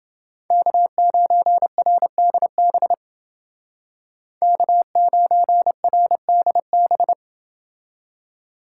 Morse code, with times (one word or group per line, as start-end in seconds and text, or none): 0.5-2.9 K9RDB
4.4-7.1 K9RDB